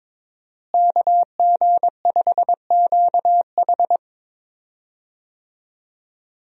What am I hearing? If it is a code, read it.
Morse code, heard as a keyed tone, about 21 wpm, 700 Hz. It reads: KG5QH
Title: USA Call Signs - 5 Characters - Repeated 3x - 22wpm